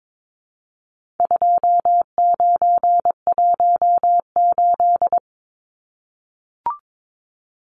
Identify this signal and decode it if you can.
Morse code, heard as a keyed tone, about 22 wpm, 700 Hz. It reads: 2918
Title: Numbers - 4-Digit 22wpm